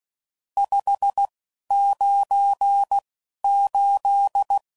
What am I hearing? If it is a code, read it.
Morse code, heard as a keyed tone, about 16 wpm, spaced as for 11 wpm, 800 Hz. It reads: 598
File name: CW Code Groups @10WPM / RSTs / Various RST No's